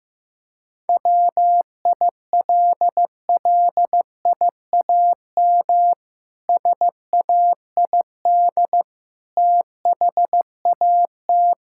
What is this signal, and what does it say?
Morse code, heard as a keyed tone, about 15 wpm, 700 Hz. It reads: WILLIAM SAID THAT
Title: Pangram Challenge v2 - 15wpm